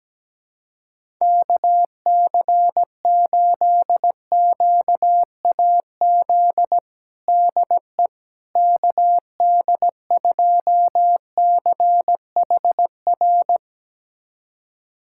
Morse code, heard as a keyed tone, about 17 wpm, 700 Hz. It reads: KC8QAZ DE KD2CHR
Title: USA Callsign DE Callsign - 6 Characters - 17wpm